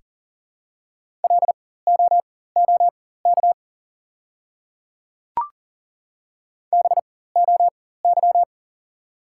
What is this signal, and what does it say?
Morse code, heard as a keyed tone, about 37 wpm, spaced as for 18 wpm, 700 Hz. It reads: LOOK BOY
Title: 4x Character Spacing - Top 200 Words - 40wpm